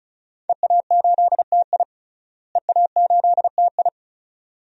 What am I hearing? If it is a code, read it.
Morse code, heard as a keyed tone, about 36 wpm, 700 Hz. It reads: EA8TI EA8TI
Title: DXCC-100 Call Signs - 5 Characters - Repeated 2x - 35wpm